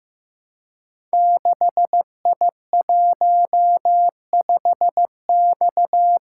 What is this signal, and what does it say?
Morse code, heard as a keyed tone, about 15 wpm, 700 Hz. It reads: 6I15X
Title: Code Groups - 5 Letter:Numbers - 15wpm